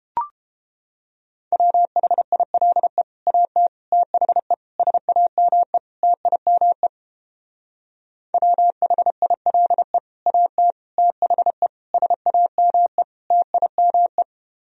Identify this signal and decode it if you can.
Morse code, heard as a keyed tone, about 31 wpm, 700 Hz. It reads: WHILE AT THE SAME TIME WHILE AT THE SAME TIME